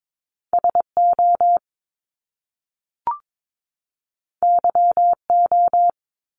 Morse code, heard as a keyed tone, about 22 wpm, 700 Hz. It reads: SO YO
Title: Top Two Letter Combinations in Words 22wpm